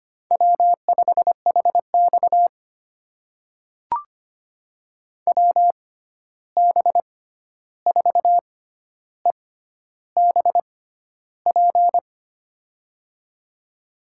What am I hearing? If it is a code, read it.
Morse code, heard as a keyed tone, about 25 wpm, spaced as for 11 wpm, 700 Hz. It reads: W5HX WB4EBP
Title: USA Call Signs - 6x Character Spacing - 25wpm